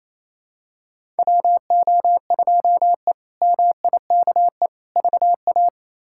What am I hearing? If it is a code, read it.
Morse code, heard as a keyed tone, about 28 wpm, 700 Hz. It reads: WO2E MIKE VA
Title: K1USN SST QSO Practice - 28wpm